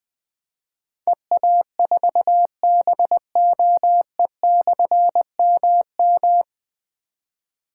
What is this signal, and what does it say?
Morse code, heard as a keyed tone, about 20 wpm, 700 Hz. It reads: EA4BOE/MM
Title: Difficult Call Signs 20wpm